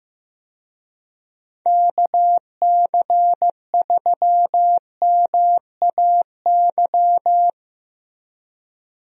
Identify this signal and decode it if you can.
Morse code, heard as a keyed tone, about 15 wpm, 700 Hz. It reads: KC3MAY
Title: USA Call Signs - 6 Characters - Repeated 3x - 15wpm